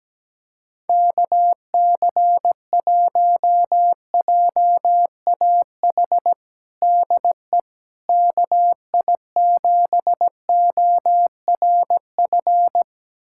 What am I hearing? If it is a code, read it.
Morse code, heard as a keyed tone, about 17 wpm, 700 Hz. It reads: KC1JAH DE KI7ORF
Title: USA Callsign DE Callsign 17wpm